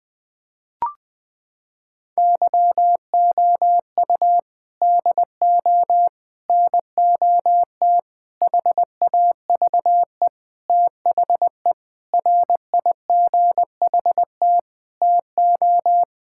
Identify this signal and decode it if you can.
Morse code, heard as a keyed tone, about 21 wpm, 700 Hz. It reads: YOU DO NOT HAVE THE RIGHT TO
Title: Sentences with 5-Word Phrases - 20wpm